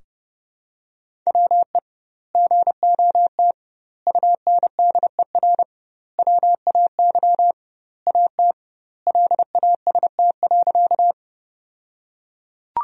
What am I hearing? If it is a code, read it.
Morse code, heard as a keyed tone, about 30 wpm, 700 Hz. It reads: WE GOT UNDER WAY AT LAST.
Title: 2x Word-Spacing - Sentences from Top 300 Words - 30wpm